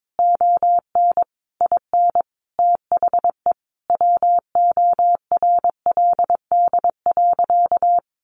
Morse code, heard as a keyed tone, about 22 wpm, 700 Hz. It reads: ON IN THE WORLD.
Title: Sentences from Top 200 Words 22wpm